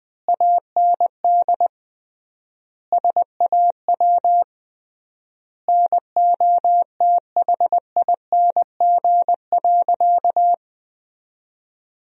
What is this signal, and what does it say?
Morse code, heard as a keyed tone, about 20 wpm, 700 Hz. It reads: AND SAW NOTHING.
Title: 3x Word-Spacing - Sentences from Top 500 Words - 20wpm